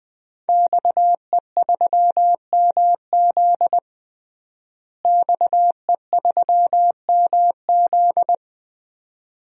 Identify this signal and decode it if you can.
Morse code, heard as a keyed tone, about 20 wpm, 700 Hz. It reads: XE3MZ XE3MZ
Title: DXCC-100 Call Signs - 5 Characters - Repeated 2x - 20wpm